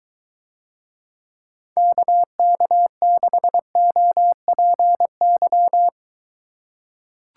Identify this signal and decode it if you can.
Morse code, heard as a keyed tone, about 23 wpm, 700 Hz. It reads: KK6OPY